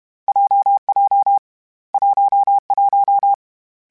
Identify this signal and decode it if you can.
Morse code, heard as a keyed tone, about 32 wpm, 800 Hz. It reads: JJ 11
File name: Variable Speed / Variable Speed-mp3s / WPJ WPJ1 sp @30